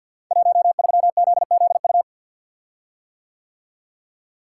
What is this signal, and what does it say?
Morse code, heard as a keyed tone, about 52 wpm, 700 Hz. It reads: 13ZZU
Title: Code Groups - 5 Letter:Numbers - 50wpm